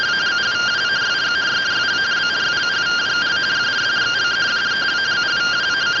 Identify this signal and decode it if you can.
Two-tone teleprinter signal, about 170 Hz shift.